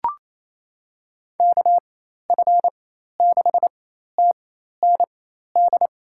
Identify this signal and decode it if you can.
Morse code, heard as a keyed tone, about 28 wpm, spaced as for 13 wpm, 700 Hz. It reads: KF6TND
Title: USA Call Signs - 6 Characters - 4x Character Spacing - 28wpm